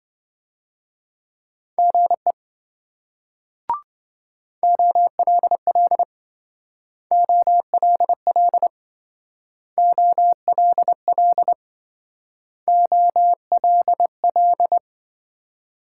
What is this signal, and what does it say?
Morse code, heard as a keyed tone, about 25 wpm, 700 Hz. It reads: GE OLL OLL OLL OLL